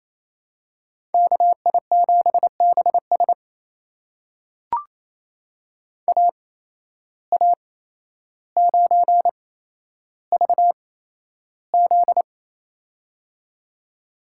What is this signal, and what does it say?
Morse code, heard as a keyed tone, about 28 wpm, spaced as for 12 wpm, 700 Hz. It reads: KI7BS AA9VZ
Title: USA Call Signs - 5 Characters - 8x Character Spacing - 28wpm